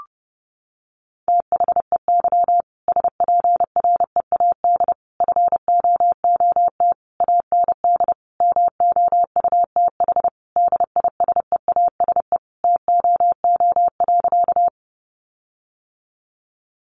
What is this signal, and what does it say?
Morse code, heard as a keyed tone, about 30 wpm, 700 Hz. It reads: THEY SPREAD FOOT AND MOUTH DISEASE TOO.
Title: Sentences from Top 1500 Words 30wpm